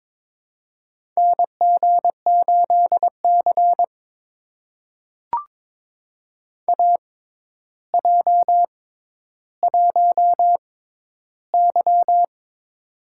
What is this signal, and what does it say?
Morse code, heard as a keyed tone, about 22 wpm, spaced as for 11 wpm, 700 Hz. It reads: NG8C AJ1Y
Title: USA Call Signs - 4 Characters - 6x Character Spacing - 22wpm